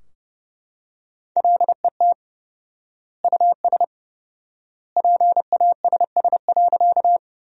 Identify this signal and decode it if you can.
Morse code, heard as a keyed tone, about 31 wpm, 700 Hz. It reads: LET US PASS.
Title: 4x Word-Spacing - Sentences from Top 400 Words - 30wpm